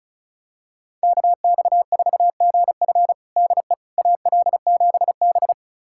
Morse code, heard as a keyed tone, about 34 wpm, 700 Hz. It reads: KX4GF DE AL7B
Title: USA Callsign DE Callsign 35wpm